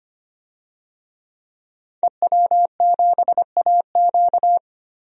Morse code, heard as a keyed tone, about 25 wpm, 700 Hz. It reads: EW7AQ
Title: DXCC-100 Call Signs - 5 Characters - 25wpm